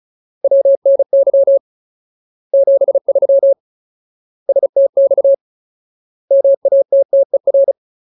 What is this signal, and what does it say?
Morse code, heard as a keyed tone, about 34 wpm, 550 Hz. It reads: WNY 73 STX MATTER
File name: Ultimate Word List / 35WPM Ultimate